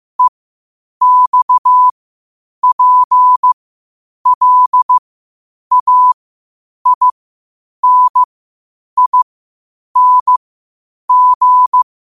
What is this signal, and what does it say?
Morse code, heard as a keyed tone, about 15 wpm, spaced as for 8 wpm, 1000 Hz. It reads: EXPLAINING
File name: qotdfiles / 8wpm